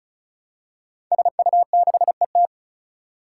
Morse code, heard as a keyed tone, about 36 wpm, 700 Hz. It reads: SU6ET
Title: Five Challenging Letter-Number Combinations 35wpm